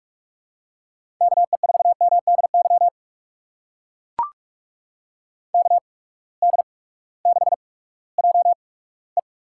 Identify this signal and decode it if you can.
Morse code, heard as a keyed tone, about 44 wpm, spaced as for 20 wpm, 700 Hz. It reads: KE4MDY KD6JE
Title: USA Call Signs - 6 Characters - 8x Character Spacing - 45wpm